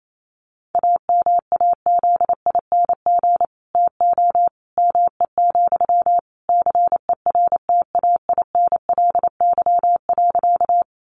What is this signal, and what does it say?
Morse code, heard as a keyed tone, about 28 wpm, 700 Hz. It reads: AMAZING TO ME, CERTAINLY.
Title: Sentences from Top 1500 Words 28wpm